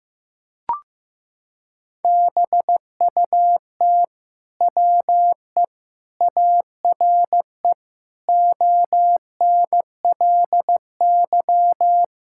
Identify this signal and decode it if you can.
Morse code, heard as a keyed tone, about 16 wpm, 700 Hz. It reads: BUT WE ARE ONLY
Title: Sentences from Top 1000 Words - Review Part 5 - 15wpm